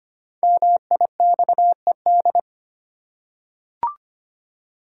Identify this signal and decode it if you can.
Morse code, heard as a keyed tone, about 25 wpm, 700 Hz. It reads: MIXED
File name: 5 Letter Words - First 3 Letters Given - 25wpm